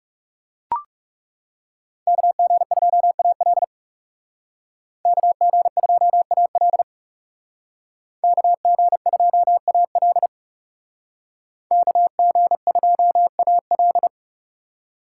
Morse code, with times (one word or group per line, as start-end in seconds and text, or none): 2.1-3.6 KG2AL
5.1-6.8 KG2AL
8.2-10.3 KG2AL
11.7-14.1 KG2AL